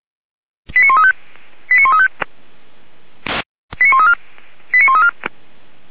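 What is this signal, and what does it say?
Five-tone selective call (ZVEI), heard as quick runs of five tones, tones of about 80 ms. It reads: 89135 89135 89134 89134
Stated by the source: ZVEI five-tone call 89131 received on SDR.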